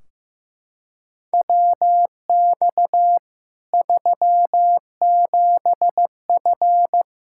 Morse code, heard as a keyed tone, about 15 wpm, 700 Hz. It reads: WX 37F
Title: Practice QSO - Exchange 2 - Info - 15wpm